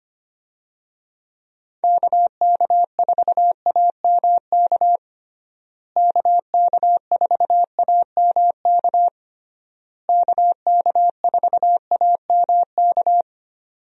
Morse code, heard as a keyed tone, about 25 wpm, 700 Hz. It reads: KK4AMK KK4AMK KK4AMK
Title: USA Call Signs - 6 Characters - Repeated 3x - 25wpm